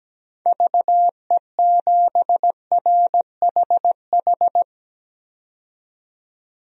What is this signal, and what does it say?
Morse code, heard as a keyed tone, about 17 wpm, 700 Hz. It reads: VE7RHH
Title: DXCC-100 Call Signs - 6 Characters - 17wpm